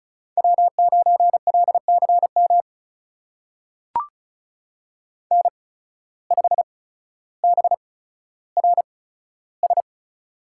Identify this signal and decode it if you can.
Morse code, heard as a keyed tone, about 36 wpm, spaced as for 15 wpm, 700 Hz. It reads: W9LCM N5BRS
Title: USA Call Signs - 5 Characters - 8x Character Spacing - 35wpm